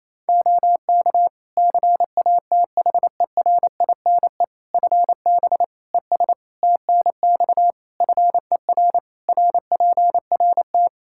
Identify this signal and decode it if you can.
Morse code, heard as a keyed tone, about 28 wpm, 700 Hz. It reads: OK CATHERINE FB ES TNX FER RPRT
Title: Practice QSO - Exchange 2 - Info - 28wpm